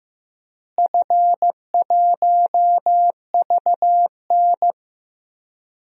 Morse code, heard as a keyed tone, about 15 wpm, 700 Hz. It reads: F1VN
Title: DXCC-100 Call Signs - 4 Characters - Repeated 3x - 15wpm